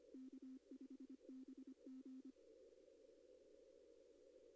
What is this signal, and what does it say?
Morse code, heard as a keyed tone, about 25 wpm, 260 Hz. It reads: K5BG
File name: contests / arrl-160-k5tr-op-w5tn -2200